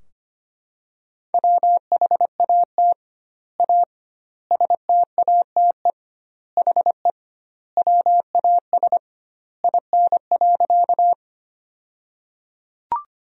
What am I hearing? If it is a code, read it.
Morse code, heard as a keyed tone, about 25 wpm, 700 Hz. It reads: WHAT A STATE HE WAS IN.
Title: 2x Word-Spacing - Sentences from Top 400 Words - 25wpm